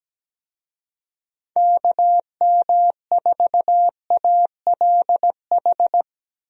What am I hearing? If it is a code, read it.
Morse code, heard as a keyed tone, about 17 wpm, 700 Hz. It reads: KM4ALH
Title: USA Call Signs - 6 Characters - 17wpm